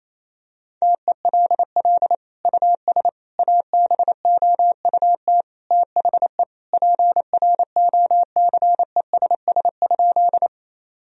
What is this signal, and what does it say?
Morse code, heard as a keyed tone, about 28 wpm, 700 Hz. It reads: TELL US ABOUT THE PROCESS?
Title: Sentences from Top 1000 Words - Review Part 3 - 28wpm